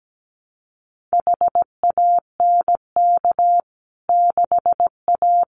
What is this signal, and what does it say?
Morse code, heard as a keyed tone, about 17 wpm, 700 Hz. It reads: HANK 6A
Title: CWT Exchange 17wpm